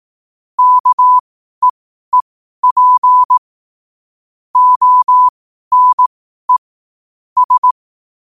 Morse code, heard as a keyed tone, about 18 wpm, spaced as for 12 wpm, 1000 Hz. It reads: KEEP ONE S
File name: qotdfiles / 12wpm